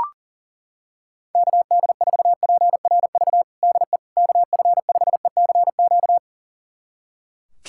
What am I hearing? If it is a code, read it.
Morse code, heard as a keyed tone, about 40 wpm, 700 Hz. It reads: KD4PRV DE KF5ECQ
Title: USA Callsign DE Callsign - 6 Characters - 40wpm